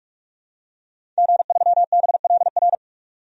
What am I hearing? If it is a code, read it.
Morse code, heard as a keyed tone, about 41 wpm, 700 Hz. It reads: G3BLR
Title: DXCC-100 Call Signs - 5 Characters - Repeated 3x - 45wpm